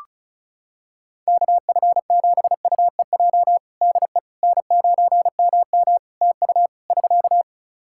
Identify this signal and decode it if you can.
Morse code, heard as a keyed tone, about 36 wpm, 700 Hz. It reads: KF7UEJ DE N9MM TU <SK>
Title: Practice QSO - Exchange 4 - Ending - 35wpm